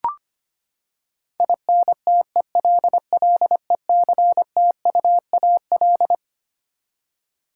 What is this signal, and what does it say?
Morse code, heard as a keyed tone, about 25 wpm, 700 Hz. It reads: INTELLECTUAL
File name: Twelve Letter Words - 25wpm